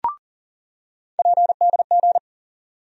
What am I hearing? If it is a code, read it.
Morse code, heard as a keyed tone, about 40 wpm, 700 Hz. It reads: PDG
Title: USA Call Sign Postfixes - 3 Letters 40wpm